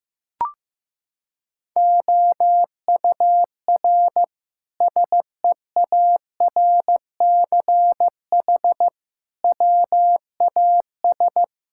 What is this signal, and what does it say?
Morse code, heard as a keyed tone, about 16 wpm, 700 Hz. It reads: OUR SEARCH WAS
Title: Sentences from Top 1000 Words - Review Part 5 - 15wpm